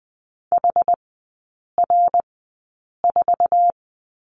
Morse code, heard as a keyed tone, about 20 wpm, spaced as for 8 wpm, 700 Hz. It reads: HR4
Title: 121 Lesson 20e - Number 3 - Three Characters 20wpm